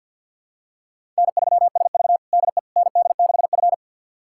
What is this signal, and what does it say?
Morse code, heard as a keyed tone, about 52 wpm, 700 Hz. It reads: N3SV DE ND6F
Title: USA Callsign DE Callsign - 4 Characters - 50wpm